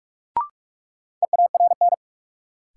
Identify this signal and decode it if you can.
Morse code, heard as a keyed tone, about 46 wpm, 700 Hz. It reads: EARN
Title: Sentences from Top 1500 Words - New Word Drill - 45wpm